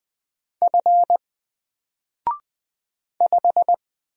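Morse code, heard as a keyed tone, about 20 wpm, 700 Hz. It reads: F 5
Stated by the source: Single letter and number Morse drill at 20 wpm, from N4LCD.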